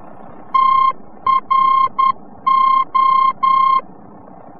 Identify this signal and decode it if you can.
Morse code, heard as a keyed tone, about 10 wpm, 1060 Hz. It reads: TRO
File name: france / mx / TRO-VOR-F